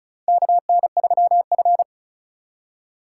Morse code, heard as a keyed tone, about 34 wpm, 700 Hz. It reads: KN3F
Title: USA Call Signs - 4 Characters - 8x Character Spacing - 35wpm